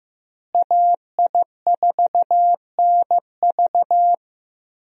Morse code, heard as a keyed tone, about 16 wpm, 700 Hz. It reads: AI4NV
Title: USA Call Signs - 5 Characters - Repeated 2x - 15wpm